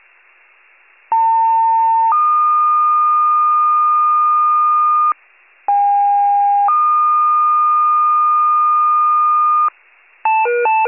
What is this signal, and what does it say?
Two-tone page, heard as a one-second tone, then a three-second tone, about 900 then 1220 Hz, about 810 then 1220 Hz. It is heard from 1.1 to 9.7 s.
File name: pager / 73 RESCUE / wcalert